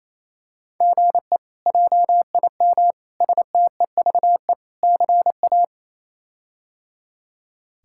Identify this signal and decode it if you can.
Morse code, heard as a keyed tone, about 27 wpm, 700 Hz. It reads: GE JIM STEVE CA
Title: K1USN SST QSO Practice - 28wpm